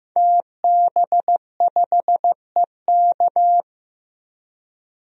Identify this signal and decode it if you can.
Morse code, heard as a keyed tone, about 15 wpm, 700 Hz. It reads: TB5EK
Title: DXCC-100 Call Signs - 5 Characters - Repeated 3x - 15wpm